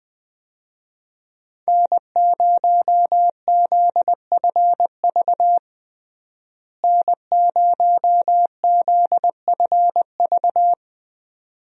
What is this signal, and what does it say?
Morse code, heard as a keyed tone, about 20 wpm, 700 Hz. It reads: N0ZFV N0ZFV
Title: USA Call Signs - 5 Characters - Repeated 2x - 20wpm